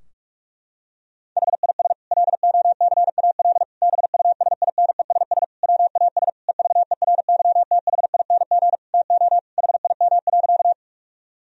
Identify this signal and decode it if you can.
Morse code, heard as a keyed tone, about 46 wpm, 700 Hz. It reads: HIS LOCAL BUSINESS WAS EVERYTHING TO HIM.
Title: Sentences from Top 1000 Words - Review Part 3 - 45wpm